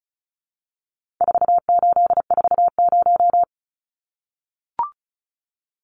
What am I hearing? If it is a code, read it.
Morse code, heard as a keyed tone, about 34 wpm, 700 Hz. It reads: 4840
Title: Numbers - 4-Digit 35wpm